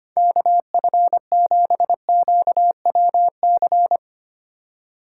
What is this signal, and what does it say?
Morse code, heard as a keyed tone, about 25 wpm, 700 Hz. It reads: KF7QWC
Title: USA Call Signs - 6 Characters - 2x Character Spacing - 25wpm